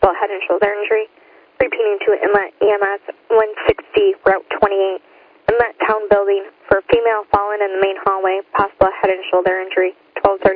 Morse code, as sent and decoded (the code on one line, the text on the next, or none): none